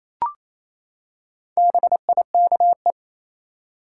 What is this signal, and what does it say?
Morse code, heard as a keyed tone, about 28 wpm, 700 Hz. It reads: BIKE
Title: Four Letter Words - 28wpm